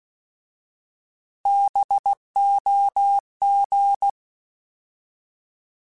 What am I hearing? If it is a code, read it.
Morse code, heard as a keyed tone, about 16 wpm, 800 Hz. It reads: BOG
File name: Alphabetical / Alphabetical mp3s / Bs @15